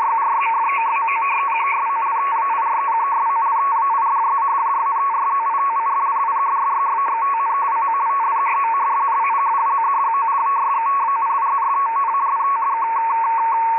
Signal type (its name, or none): two-tone teleprinter signal